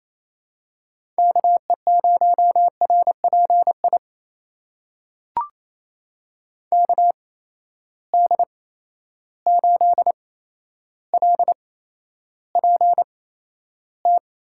Morse code, heard as a keyed tone, about 28 wpm, spaced as for 12 wpm, 700 Hz. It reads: KE0RPI KD8LPT
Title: USA Call Signs - 6 Characters - 8x Character Spacing - 28wpm